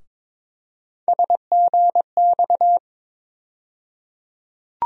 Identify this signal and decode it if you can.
Morse code, heard as a keyed tone, about 22 wpm, 700 Hz. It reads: SGX
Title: USA Call Sign Postfixes - 3 Letters 22wpm